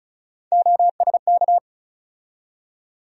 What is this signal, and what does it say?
Morse code, heard as a keyed tone, about 36 wpm, 700 Hz. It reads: OSK
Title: USA Call Sign Postfixes - 3 Letters 35wpm